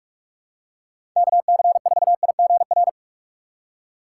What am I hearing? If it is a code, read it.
Morse code, heard as a keyed tone, about 47 wpm, 700 Hz. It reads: KC4IGR
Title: USA Call Signs - 6 Characters - 4x Character Spacing - 45wpm